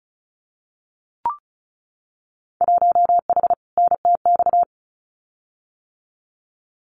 Morse code, heard as a keyed tone, about 34 wpm, 700 Hz. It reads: 1H NTX
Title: Winter Field Day Exchanges 35wpm